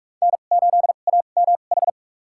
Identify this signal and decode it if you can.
Morse code, heard as a keyed tone, about 41 wpm, spaced as for 31 wpm, 690 Hz. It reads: N8AMH
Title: USA Call Signs - 5 Characters - 2x Character Spacing - 45wpm